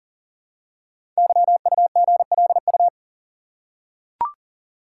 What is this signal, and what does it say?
Morse code, heard as a keyed tone, about 43 wpm, 690 Hz. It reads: YUGLU
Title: Code Groups - 5 Letter:Numbers - 40wpm